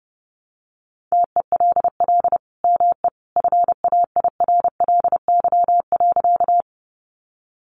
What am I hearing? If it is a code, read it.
Morse code, heard as a keyed tone, about 30 wpm, 700 Hz. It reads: TELL ME FAIRLY.
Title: Sentences from Top 1100 Words 30wpm